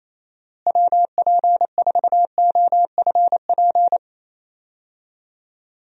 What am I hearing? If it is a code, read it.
Morse code, heard as a keyed tone, about 28 wpm, 700 Hz. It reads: WP4OFP
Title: USA Call Signs - 6 Characters - Repeated 2x - 28wpm